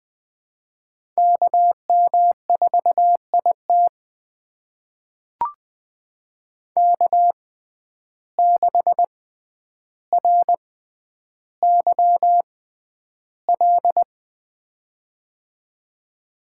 Morse code, heard as a keyed tone, about 20 wpm, spaced as for 10 wpm, 700 Hz. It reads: KM4IT K6RYL
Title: USA Call Signs - 5 Characters - 6x Character Spacing - 20wpm